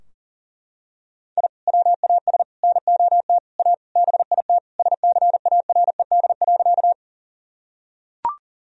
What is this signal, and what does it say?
Morse code, heard as a keyed tone, about 40 wpm, 700 Hz. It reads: I WAS NOT A BIT SCARED.
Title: Sentences from Top 1000 Words - Review Part 4 - 40wpm